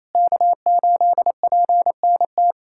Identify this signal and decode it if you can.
Morse code, heard as a keyed tone, about 27 wpm, 700 Hz. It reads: K8PNT
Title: USA Callsign DE Callsign - 5 Characters - 28wpm